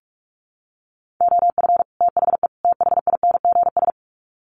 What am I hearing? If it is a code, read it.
Morse code, heard as a keyed tone, about 44 wpm, 700 Hz. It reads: OF THE THINGS
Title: Sets of 3 Words 45wpm